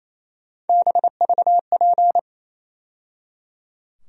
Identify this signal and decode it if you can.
Morse code, heard as a keyed tone, about 28 wpm, 700 Hz. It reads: BVP
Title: USA Call Sign Postfixes - 3 Letters 28wpm